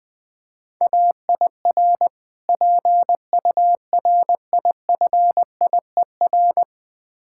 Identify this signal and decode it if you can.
Morse code, heard as a keyed tone, about 22 wpm, 700 Hz. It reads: AIR PURIFIER
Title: Difficult Words - Speed-Racing with Hints - 20wpm